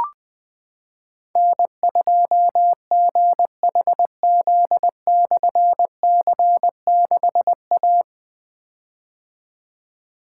Morse code, heard as a keyed tone, about 20 wpm, 700 Hz. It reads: N2GHZ/C6A
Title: Difficult Call Signs 20wpm